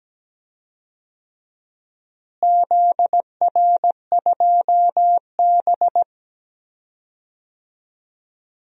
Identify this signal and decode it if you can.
Morse code, heard as a keyed tone, about 17 wpm, 700 Hz. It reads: ZR2B
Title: DXCC-100 Call Signs - 4 Characters - Repeated 3x - 17wpm